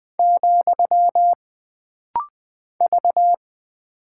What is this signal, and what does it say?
Morse code, heard as a keyed tone, about 20 wpm, 700 Hz. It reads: , V
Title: Single Character (Letter:Number:Punctuation:Prosign) - Rapid-Fire - 20wpm